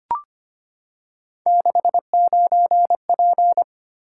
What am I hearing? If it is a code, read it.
Morse code, heard as a keyed tone, about 25 wpm, 700 Hz. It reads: 69P
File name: Three Letter-Number Combinations - 25wpm